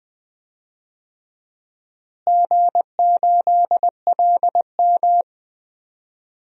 Morse code, heard as a keyed tone, about 20 wpm, 700 Hz. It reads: G8LM
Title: DXCC-100 Call Signs - 4 Characters - Repeated 3x - 20wpm